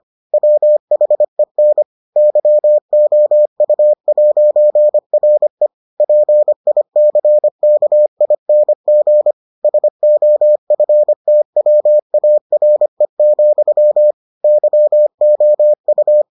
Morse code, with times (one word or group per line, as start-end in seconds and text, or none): 0.3-1.8 WHEN
2.2-5.7 YOU'RE
6.0-9.3 PICKING
9.6-14.1 SOFTWARE,
14.4-16.2 YOU